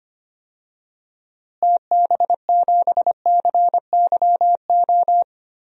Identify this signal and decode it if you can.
Morse code, heard as a keyed tone, about 25 wpm, 700 Hz. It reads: TB7CYO
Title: DXCC-100 Call Signs - 6 Characters - Repeated 3x - 25wpm